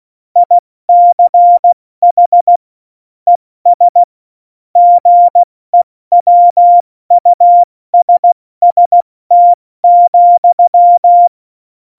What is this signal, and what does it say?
Morse code, heard as a keyed tone, about 16 wpm, spaced as for 14 wpm, 700 Hz. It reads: ICH ES GEWUSST,
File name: Morsekurs / Witze / Witz 07 Juli 2020, 70 F80